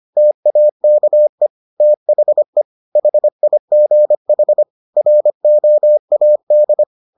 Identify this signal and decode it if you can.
Morse code, heard as a keyed tone, about 25 wpm, 600 Hz. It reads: TAKE THE HIGH ROAD